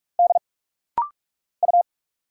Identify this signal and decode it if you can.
Morse code, heard as a keyed tone, about 46 wpm, 700 Hz. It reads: D U
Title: Single Letters - Rapid-Fire 45wpm